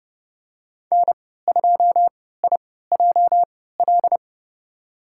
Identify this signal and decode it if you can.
Morse code, heard as a keyed tone, about 30 wpm, spaced as for 17 wpm, 700 Hz. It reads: N2IJL
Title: USA Call Signs - 5 Characters - 3x Character Spacing - 30wpm